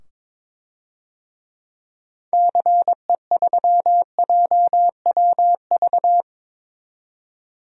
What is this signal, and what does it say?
Morse code, heard as a keyed tone, about 22 wpm, 700 Hz. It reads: CE3JWV
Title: DXCC-100 Call Signs - 6 Characters - 22wpm